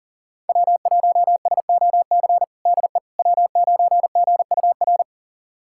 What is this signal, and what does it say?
Morse code, heard as a keyed tone, about 40 wpm, 700 Hz. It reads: W1SOC DE W9GUR
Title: USA Callsign DE Callsign - 5 Characters - 40wpm